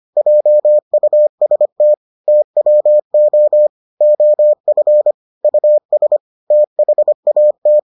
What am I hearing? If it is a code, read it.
Morse code, heard as a keyed tone, about 25 wpm, 600 Hz. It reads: JUST TWO OF US THAT